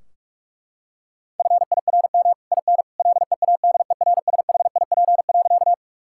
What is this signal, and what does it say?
Morse code, heard as a keyed tone, about 47 wpm, 700 Hz. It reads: FIRM IN LEADERSHIP.
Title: Sentences from Top 1600 Words 45wpm